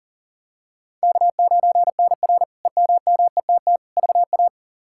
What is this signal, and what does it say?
Morse code, heard as a keyed tone, about 37 wpm, 700 Hz. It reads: K9NR EMMETT VA
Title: K1USN SST QSO Practice - 40wpm